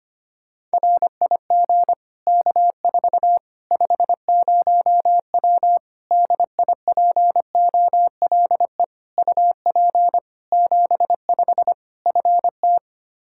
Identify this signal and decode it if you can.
Morse code, heard as a keyed tone, about 25 wpm, 700 Hz. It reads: RIG K4 50W DIPOLE UP 75 FT
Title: Practice QSO - Exchange 2 - Info - 25wpm